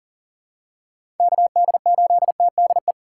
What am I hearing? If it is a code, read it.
Morse code, heard as a keyed tone, about 40 wpm, 700 Hz. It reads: KD8TDE
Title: USA Call Signs - 6 Characters - 3x Character Spacing - 40wpm